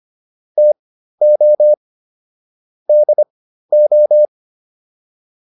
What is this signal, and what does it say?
Morse code, heard as a keyed tone, about 25 wpm, spaced as for 13 wpm, 600 Hz. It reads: TO DO